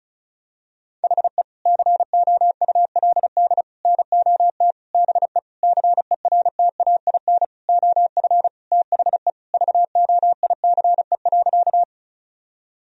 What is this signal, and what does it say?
Morse code, heard as a keyed tone, about 34 wpm, 700 Hz. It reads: HE COULD NOT BE CERTAIN OF THE VOICE.
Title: Sentences from Top 1000 Words - Review Part 2 - 35wpm